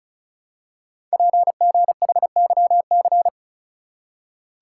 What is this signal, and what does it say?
Morse code, heard as a keyed tone, about 36 wpm, 700 Hz. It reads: PGHYC
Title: Code Groups - 5 Letters - 35wpm